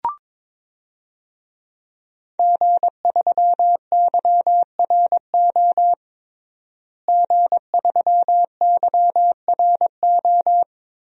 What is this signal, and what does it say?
Morse code, heard as a keyed tone, about 22 wpm, 700 Hz. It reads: G3YRO G3YRO
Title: DXCC-100 Call Signs - 5 Characters - Repeated 3x - 22wpm